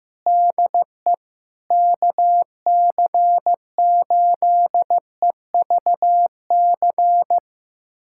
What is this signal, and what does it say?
Morse code, heard as a keyed tone, about 15 wpm, 700 Hz. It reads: DE KC8EVC
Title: USA Callsign DE Callsign 15wpm